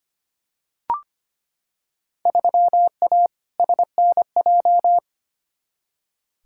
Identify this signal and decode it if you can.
Morse code, heard as a keyed tone, about 25 wpm, 700 Hz. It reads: 3A SNJ
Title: Field Day Exchanges - 25wpm